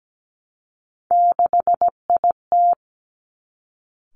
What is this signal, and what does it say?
Morse code, heard as a keyed tone, about 17 wpm, 700 Hz. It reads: 6IT
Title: Three Challenging Letter-Number Combinations 17wpm